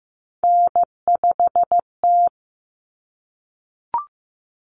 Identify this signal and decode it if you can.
Morse code, heard as a keyed tone, about 15 wpm, 700 Hz. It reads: N5T
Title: Common Cut Numbers - 3-Digit 15wpm